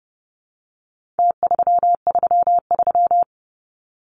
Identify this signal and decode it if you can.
Morse code, heard as a keyed tone, about 30 wpm, 700 Hz. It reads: T333
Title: Numbers - Contest Style - 4-Digit 30wpm